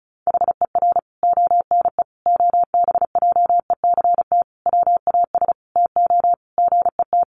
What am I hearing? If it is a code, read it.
Morse code, heard as a keyed tone, about 34 wpm, 700 Hz. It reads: HER ONE OBJECT WAS TO GET
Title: Sentences from Top 1500 Words 35wpm